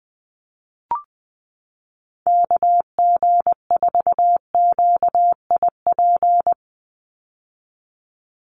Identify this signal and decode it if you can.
Morse code, heard as a keyed tone, about 20 wpm, 700 Hz. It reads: KG4QIP
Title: USA Call Signs 20wpm